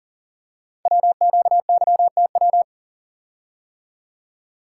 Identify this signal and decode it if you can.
Morse code, heard as a keyed tone, about 39 wpm, 700 Hz. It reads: WQYTW
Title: Code Groups - 5 Letters - 40wpm